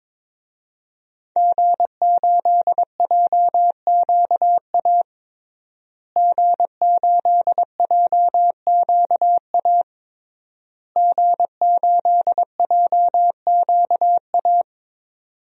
Morse code, heard as a keyed tone, about 22 wpm, 700 Hz. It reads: G8JQA G8JQA G8JQA
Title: DXCC-100 Call Signs - 5 Characters - Repeated 3x - 22wpm